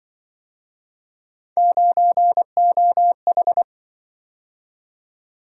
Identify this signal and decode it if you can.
Morse code, heard as a keyed tone, about 23 wpm, 700 Hz. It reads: 9OH
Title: Speed-Racing - Three Letter-Number Combinations - 20wpm